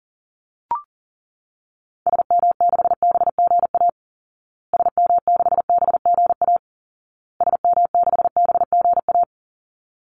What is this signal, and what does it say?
Morse code, heard as a keyed tone, about 40 wpm, 700 Hz. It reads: SM6BGA SM6BGA SM6BGA
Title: Call Signs Repeated 2x - 40wpm